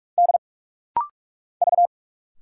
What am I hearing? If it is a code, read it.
Morse code, heard as a keyed tone, about 46 wpm, 700 Hz. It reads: D V
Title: Single Letters - Rapid-Fire 45wpm